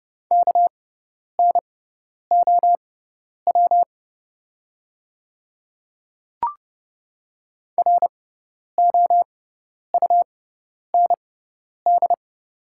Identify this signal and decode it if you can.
Morse code, heard as a keyed tone, about 30 wpm, spaced as for 10 wpm, 690 Hz. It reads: KNOW ROUND
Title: 6x Character Spacing - Top 200 Words - 30wpm